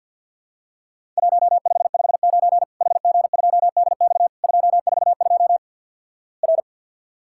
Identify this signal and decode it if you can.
Morse code, heard as a keyed tone, about 52 wpm, 700 Hz. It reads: 1559 HG2DX 242 R
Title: Worked All Europe DX Contest - QTC Exchange with Repeats - 50wpm